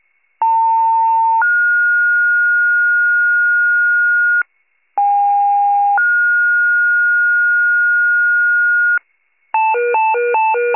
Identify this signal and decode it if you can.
Two-tone page, heard as a one-second tone, then a three-second tone, about 900 then 1430 Hz, about 810 then 1430 Hz. It is heard from 0.4 to 9.0 s.
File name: pager / 78 FIRE / wcalert